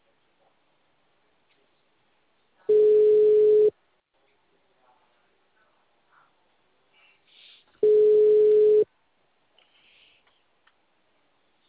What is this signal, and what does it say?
Ringback tone (Europe, 425 Hz).